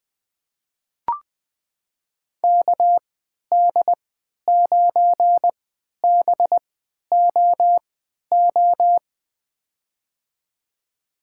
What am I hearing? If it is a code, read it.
Morse code, heard as a keyed tone, about 20 wpm, spaced as for 11 wpm, 700 Hz. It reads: KD9BOO
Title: USA Call Signs - 6 Characters - 3x Character Spacing - 20wpm